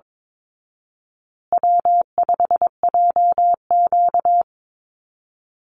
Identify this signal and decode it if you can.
Morse code, heard as a keyed tone, about 22 wpm, 700 Hz. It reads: W5JQ
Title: Call Signs Repeated 2x - 22wpm